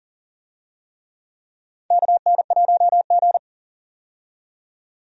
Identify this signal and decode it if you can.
Morse code, heard as a keyed tone, about 40 wpm, 700 Hz. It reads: KN1G
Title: USA Call Signs - 4 Characters - 40wpm